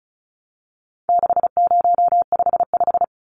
Morse code, heard as a keyed tone, about 34 wpm, 700 Hz. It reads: 6055
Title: Numbers - 4-Digit 35wpm